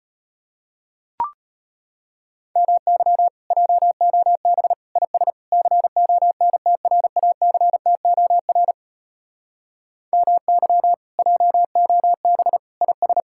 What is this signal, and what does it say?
Morse code, heard as a keyed tone, about 33 wpm, 700 Hz. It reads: MY JOB IS CONTRACTOR MY JOB IS